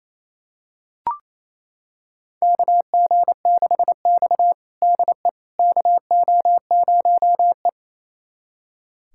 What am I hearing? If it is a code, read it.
Morse code, heard as a keyed tone, about 28 wpm, 700 Hz. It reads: KG6X DE KO0E
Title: USA Callsign DE Callsign - 4 Characters - 28wpm